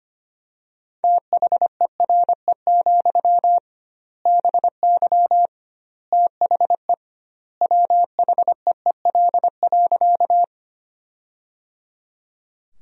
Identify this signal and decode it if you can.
Morse code, heard as a keyed tone, about 25 wpm, 700 Hz. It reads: THERE, BY THE WHEEL.
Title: 2x Word-Spacing - Sentences from Top 500 Words - 25wpm